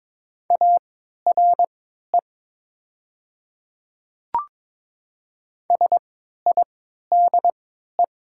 Morse code, heard as a keyed tone, about 22 wpm, spaced as for 13 wpm, 690 Hz. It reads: ARE SIDE
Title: 3x Character Spacing - Top 100 Words - 22wpm